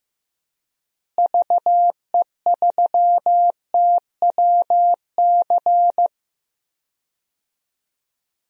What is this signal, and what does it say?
Morse code, heard as a keyed tone, about 15 wpm, 700 Hz. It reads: VE3TWC
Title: DXCC-100 Call Signs - 6 Characters - Repeated 3x - 15wpm